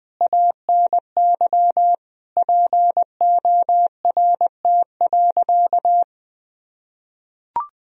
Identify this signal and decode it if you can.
Morse code, heard as a keyed tone, about 21 wpm, 700 Hz. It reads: ANY PORT.
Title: Sentences from Top 1000 Words - Review Part 2 - 20wpm